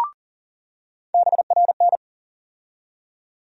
Morse code, heard as a keyed tone, about 40 wpm, 700 Hz. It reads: BRN
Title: USA Call Sign Postfixes - 3 Letters 40wpm